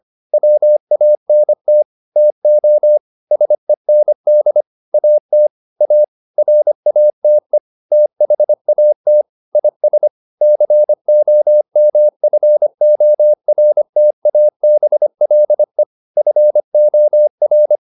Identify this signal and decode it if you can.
Morse code, heard as a keyed tone, about 25 wpm, 600 Hz. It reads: WANT TO SEND AT A RATE THAT IS COMFORTABLE FOR